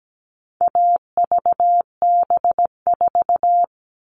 Morse code, heard as a keyed tone, about 17 wpm, 700 Hz. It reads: AVB4
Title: All Cut Numbers - 4-Digit 17wpm